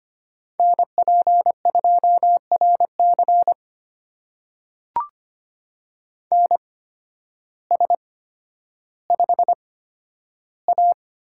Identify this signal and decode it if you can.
Morse code, heard as a keyed tone, about 25 wpm, spaced as for 12 wpm, 700 Hz. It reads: NP2RC NS5A
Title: USA Call Signs - 8x Character Spacing - 25wpm